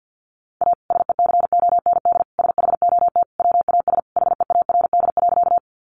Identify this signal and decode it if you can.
Morse code, heard as a keyed tone, about 50 wpm, 700 Hz. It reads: A SECOND SHOT WAS HEARD.
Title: Sentences from Top 1100 Words 50wpm